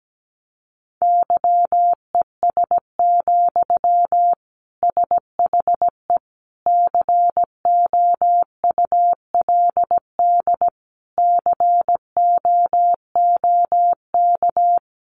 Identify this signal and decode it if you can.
Morse code, heard as a keyed tone, about 17 wpm, 700 Hz. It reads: YES, SHE COULD COOK
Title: Sentences from Top 700 Words 17wpm